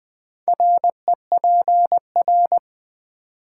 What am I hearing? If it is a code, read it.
Morse code, heard as a keyed tone, about 20 wpm, 700 Hz. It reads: REPR
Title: Build A Word - 4 Letters At A Time - 12-Letter Words - 20wpm